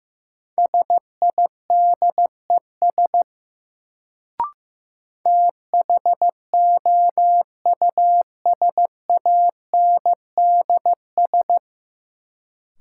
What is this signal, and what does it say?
Morse code, heard as a keyed tone, about 15 wpm, 700 Hz. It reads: SIDES THOUSANDS
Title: Sets of 4 Words - New Word Drill - 15wpm